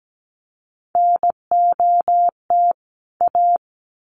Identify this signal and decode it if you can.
Morse code, heard as a keyed tone, about 17 wpm, 700 Hz. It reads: NOT A
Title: Sets of 2 Words 17wpm